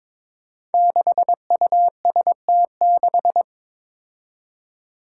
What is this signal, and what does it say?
Morse code, heard as a keyed tone, about 22 wpm, 700 Hz. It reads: 6UST6
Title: Five Challenging Letter-Number Combinations 22wpm